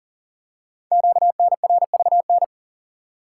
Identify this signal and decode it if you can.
Morse code, heard as a keyed tone, about 42 wpm, 690 Hz. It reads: QNRVN
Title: Code Groups - 5 Letters - 40wpm